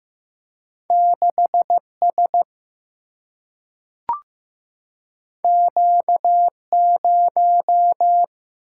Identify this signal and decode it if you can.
Morse code, heard as a keyed tone, about 15 wpm, 700 Hz. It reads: 6S Q0
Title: Single-Letter Single-Number 15wpm